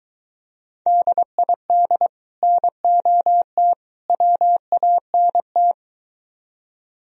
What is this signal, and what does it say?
Morse code, heard as a keyed tone, about 23 wpm, 700 Hz. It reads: DID NOT WANT